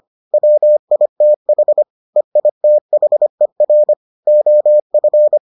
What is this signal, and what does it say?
Morse code, heard as a keyed tone, about 25 wpm, 600 Hz. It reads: WITH EITHER OF